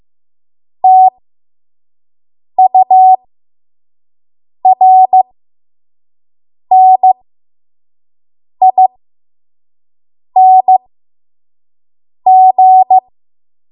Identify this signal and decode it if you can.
Morse code, heard as a keyed tone, about 15 wpm, spaced as for 5 wpm, 750 Hz. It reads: TURNING
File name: Morse / 05WPM